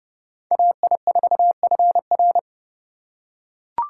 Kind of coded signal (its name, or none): Morse code